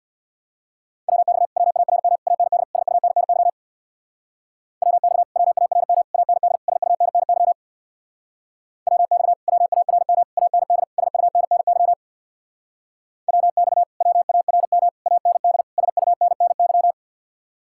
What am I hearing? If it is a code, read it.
Morse code, heard as a keyed tone, about 63 wpm, 700 Hz. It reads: WX WARM AND SUNNY WX WARM AND SUNNY WX WARM AND SUNNY WX WARM AND SUNNY